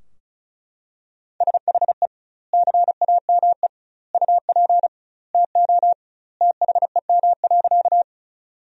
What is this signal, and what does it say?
Morse code, heard as a keyed tone, about 36 wpm, 700 Hz. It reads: SHE CAME UP TO THEM.
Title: 2x Word-Spacing - Sentences from Top 200 Words - 35wpm